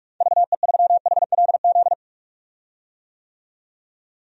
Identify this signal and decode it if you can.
Morse code, heard as a keyed tone, about 41 wpm, 700 Hz. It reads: VE3HLZ
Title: DXCC-100 Call Signs - 6 Characters - 45wpm